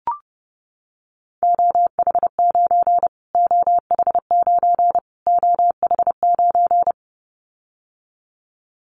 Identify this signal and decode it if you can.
Morse code, heard as a keyed tone, about 30 wpm, 700 Hz. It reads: OH9 OH9 OH9
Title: Callsign Prefixes for Top 25 DXCC - Repeated 3x - 30wpm